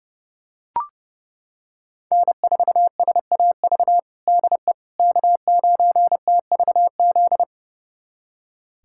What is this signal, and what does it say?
Morse code, heard as a keyed tone, about 30 wpm, 700 Hz. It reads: N4SAV DE K9TVZ
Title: USA Callsign DE Callsign - 5 Characters - 30wpm